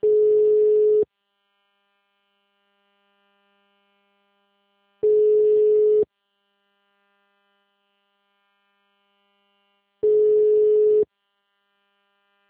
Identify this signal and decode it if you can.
Ringback tone (Europe, 425 Hz).